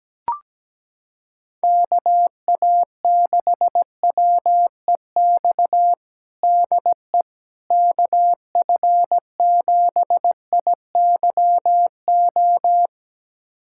Morse code, heard as a keyed tone, about 17 wpm, 700 Hz. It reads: KA6WEX DE KF7IYO
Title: USA Callsign DE Callsign - 6 Characters - 17wpm